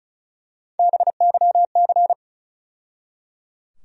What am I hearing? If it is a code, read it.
Morse code, heard as a keyed tone, about 36 wpm, 700 Hz. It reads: BYC
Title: USA Call Sign Postfixes - 3 Letters 35wpm